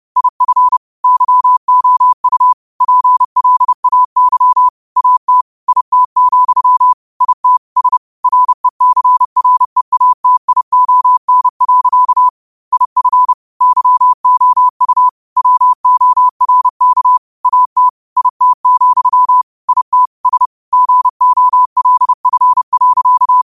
Morse code, heard as a keyed tone, about 30 wpm, 1000 Hz. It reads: IF YOU PLAY AT IT, IT S RECREATION. IF YOU WORK AT IT, IT S GOLF.
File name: qotdfiles / 30wpm